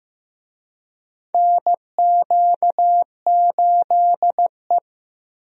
Morse code, heard as a keyed tone, about 15 wpm, 700 Hz. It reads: NQ8E
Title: USA Call Signs - 4 Characters - Repeated 2x - 15wpm